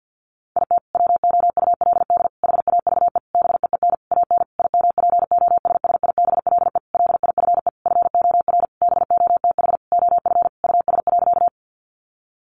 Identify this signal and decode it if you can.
Morse code, heard as a keyed tone, about 50 wpm, 700 Hz. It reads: IT WOULD HAVE BEEN AN IMPOSSIBLE LIFE FOR BOTH OF US.
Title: Sentences from Top 1100 Words 50wpm